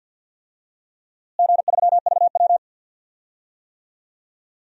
Morse code, heard as a keyed tone, about 51 wpm, 700 Hz. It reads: G3VW
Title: DXCC-100 Call Signs - 4 Characters - Repeated 2x - 50wpm